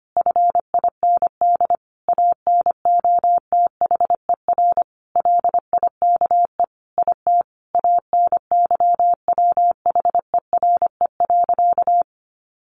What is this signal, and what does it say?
Morse code, heard as a keyed tone, about 25 wpm, 700 Hz. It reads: FIND ANOTHER LIKE IT ANYWHERE.
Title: Sentences from Top 1400 Words 25wpm